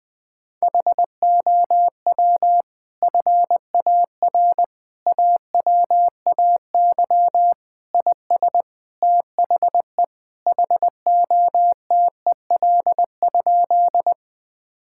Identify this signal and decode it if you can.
Morse code, heard as a keyed tone, about 20 wpm, 700 Hz. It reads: HOW FAR AWAY IS THE HOTEL?
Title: Sentences from Top 1600 Words 20wpm